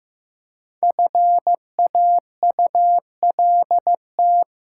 Morse code, heard as a keyed tone, about 15 wpm, 700 Hz. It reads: FAULT
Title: Five Letter Words - 15wpm